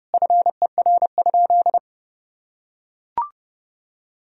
Morse code, heard as a keyed tone, about 32 wpm, 700 Hz. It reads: FER?
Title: Sentences from Top 200 Words and Abbreviations 30wpm